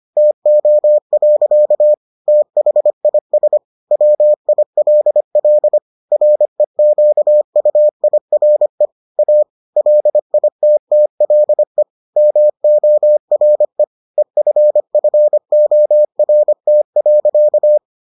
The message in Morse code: TO. THIS WILL REQUIRE A LITTLE MORE EFFORT.

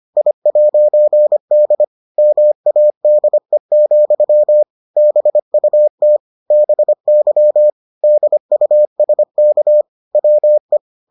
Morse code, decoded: I'D MADE, BUT BY DUSK WE

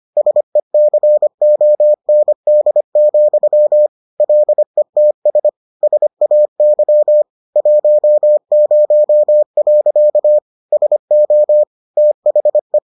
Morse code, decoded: SECOND, LETS SAY 10. SO THE

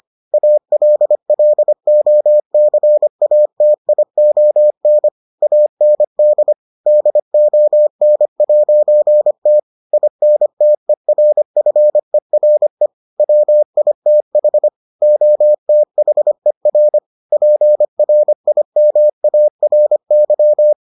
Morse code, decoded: ALLOCATION AND DON'T INTERFERE WITH OTHER PRIMARY

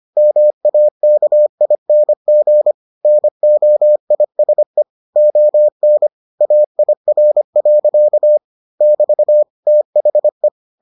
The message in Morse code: MAKING NOISE ON AIR. <BT> THE

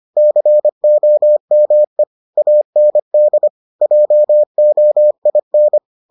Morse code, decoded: COME AND JOIN

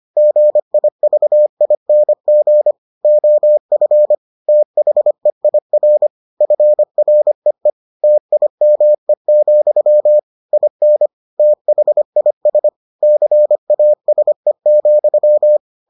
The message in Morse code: GIVING OF THEIR FREE TIME, IN THIS CASE,

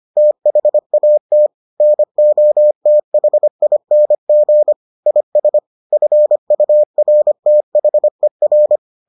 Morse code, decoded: THAT NOTHING IS FURTHER

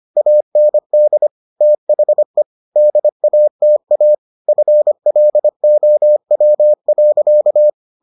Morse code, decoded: AND THE DATA FLOW.

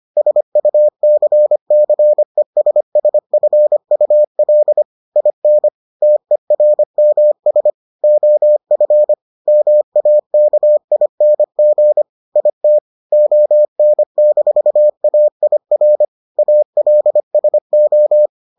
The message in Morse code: SUCCESSFUL IN TERMS OF MAKING IT ON-AIR ALSO

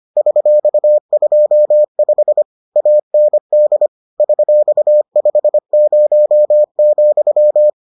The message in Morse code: $25 AND $50,

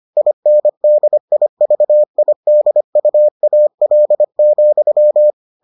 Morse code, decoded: INDIVIDUAL,

